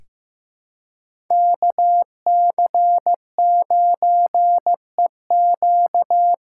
Morse code, heard as a keyed tone, about 15 wpm, 700 Hz. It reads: KC9EQ
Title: USA Call Signs - 6 Characters - 6x Character Spacing - 15wpm